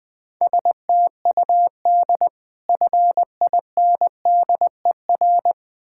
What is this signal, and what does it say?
Morse code, heard as a keyed tone, about 21 wpm, 700 Hz. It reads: STUD FINDER
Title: Difficult Words with Hints - 20wpm